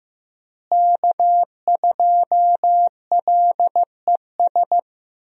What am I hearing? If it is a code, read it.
Morse code, heard as a keyed tone, about 15 wpm, 700 Hz. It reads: K2LES
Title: USA Call Signs - 5 Characters - Repeated 2x - 15wpm